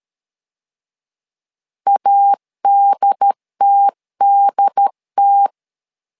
Morse code, heard as a keyed tone, about 13 wpm, 800 Hz. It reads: ADTDT
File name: LEZIONI / 2 UHDN / 2 UHD 12wpm 1540